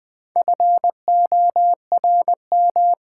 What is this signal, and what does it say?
Morse code, heard as a keyed tone, about 20 wpm, 700 Hz. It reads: FORM
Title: Sets of 4 Words - New Word Drill - 20wpm